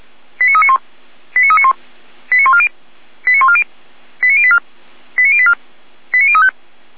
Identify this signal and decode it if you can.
Five-tone selective call (ZVEI), heard as quick runs of five tones, tones of about 70 ms. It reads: 89391 89391 89140 89140 89084 89084 89035